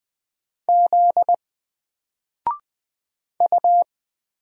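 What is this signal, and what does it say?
Morse code, heard as a keyed tone, about 20 wpm, 700 Hz. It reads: Z U
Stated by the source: Single letter and number Morse drill at 20 wpm, from N4LCD.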